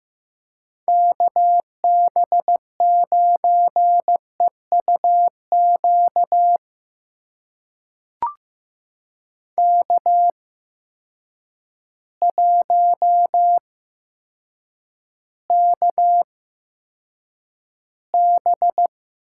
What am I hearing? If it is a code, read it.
Morse code, heard as a keyed tone, about 16 wpm, spaced as for 8 wpm, 700 Hz. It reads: KB9EUQ K1KB
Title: USA Call Signs - 8x Character Spacing - 15wpm